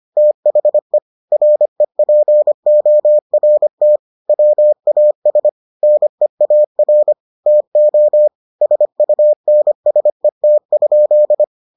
Morse code, heard as a keyed tone, about 25 wpm, 600 Hz. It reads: THE REPORT WAS NEAR TO SUNSET?